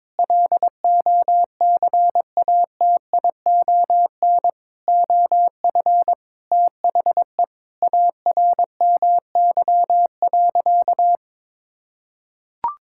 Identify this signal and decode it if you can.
Morse code, heard as a keyed tone, about 22 wpm, 700 Hz. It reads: LOCATION OF THE ARMY.
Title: Sentences from Top 1000 Words - Review Part 6 - 22wpm